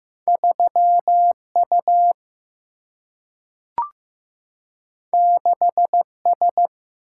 Morse code, heard as a keyed tone, about 15 wpm, 700 Hz. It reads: 3U 6S
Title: Single-Letter Single-Number 15wpm